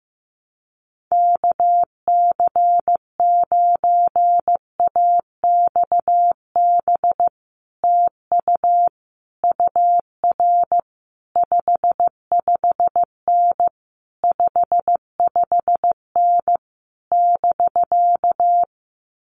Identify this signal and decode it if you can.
Morse code, heard as a keyed tone, about 15 wpm, 700 Hz. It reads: KC9AXB TU UR 55N 55N <BK>
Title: Common QSO Phrases Encore 15wpm